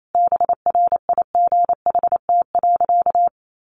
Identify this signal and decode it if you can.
Morse code, heard as a keyed tone, about 28 wpm, 700 Hz. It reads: BRIGHT.
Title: Sentences from Top 1500 Words 28wpm